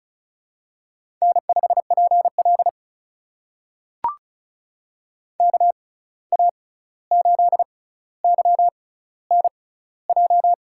Morse code, heard as a keyed tone, about 36 wpm, spaced as for 16 wpm, 700 Hz. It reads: N5PL KA8YNJ
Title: USA Call Signs - 6x Character Spacing - 35wpm